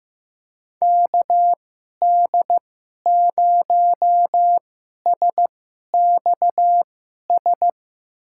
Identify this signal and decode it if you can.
Morse code, heard as a keyed tone, about 15 wpm, spaced as for 11 wpm, 700 Hz. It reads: KD0SXS
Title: USA Call Signs - 6 Characters - 2x Character Spacing - 15wpm